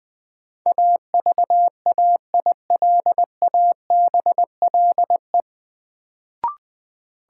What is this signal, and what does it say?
Morse code, heard as a keyed tone, about 20 wpm, 700 Hz. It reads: AVAILABLE
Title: Sentences from Top 600 Words - New Word Drill - 20wpm